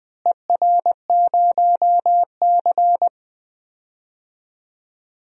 Morse code, heard as a keyed tone, about 20 wpm, 700 Hz. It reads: ER0C
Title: DXCC-100 Call Signs - 4 Characters - Repeated 3x - 20wpm